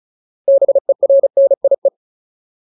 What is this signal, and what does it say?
Morse code, heard as a keyed tone, about 34 wpm, 550 Hz. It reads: BERNIE
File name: Ultimate Word List / 35WPM Ultimate